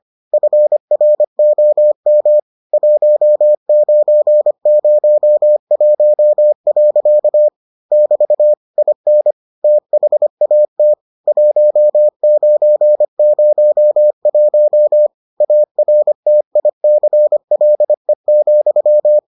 Morse code, heard as a keyed tone, about 25 wpm, 600 Hz. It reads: FROM 1901. <BT> IN THAT 1901 ARTICLE,